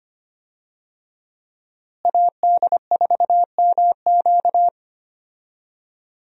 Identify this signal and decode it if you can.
Morse code, heard as a keyed tone, about 25 wpm, 700 Hz. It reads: AD4MQ
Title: USA Call Signs - 5 Characters - Repeated 3x - 25wpm